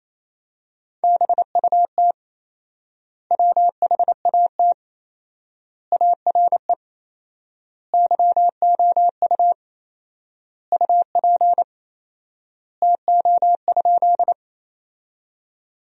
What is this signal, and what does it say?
Morse code, heard as a keyed tone, about 28 wpm, 700 Hz. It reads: BUT WHAT ARE YOU UP TO?
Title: 4x Word-Spacing - Sentences from Top 100 Words - 28wpm